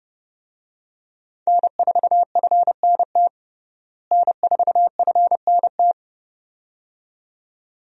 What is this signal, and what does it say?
Morse code, heard as a keyed tone, about 29 wpm, 700 Hz. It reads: N4FNT N4FNT
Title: USA Call Signs - 5 Characters - Repeated 2x - 30wpm